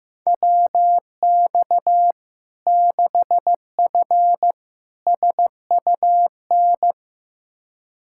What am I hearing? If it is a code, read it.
Morse code, heard as a keyed tone, about 15 wpm, 700 Hz. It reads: WX 6F SUN
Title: Practice QSO - Exchange 2 - Info - 15wpm